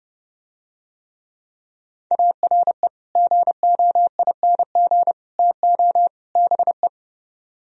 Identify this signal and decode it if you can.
Morse code, heard as a keyed tone, about 32 wpm, 700 Hz. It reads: ARE GOING TO BE